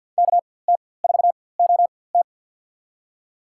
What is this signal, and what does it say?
Morse code, heard as a keyed tone, about 53 wpm, spaced as for 24 wpm, 690 Hz. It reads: KT4OT
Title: USA Call Signs - 4x Character Spacing - 50wpm